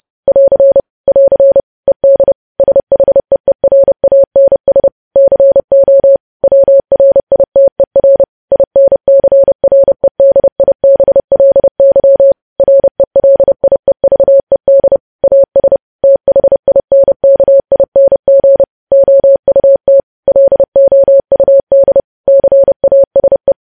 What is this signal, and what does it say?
Morse code, heard as a keyed tone, about 30 wpm, 550 Hz. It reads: <AR> <AR> ED SHEERANS CO WRITER INCREDIBLY RELIEVED AS THINKING OUT LOUD CASE
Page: news headlines in morse code at 30 wpm